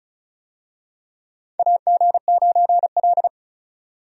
Morse code, heard as a keyed tone, about 36 wpm, 700 Hz. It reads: AG9L
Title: USA Call Signs - 4 Characters - 35wpm